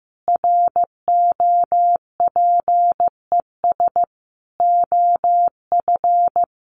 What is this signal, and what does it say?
Morse code, heard as a keyed tone, about 15 wpm, 700 Hz. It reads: ROPES OF
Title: A Sound Of Thunder 15wpm